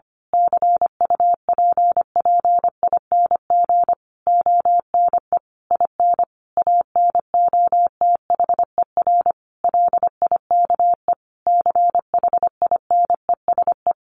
Morse code, heard as a keyed tone, about 25 wpm, 700 Hz. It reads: CUPPING ONE IN ANOTHER LIKE CHINESE
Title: A Sound Of Thunder 25wpm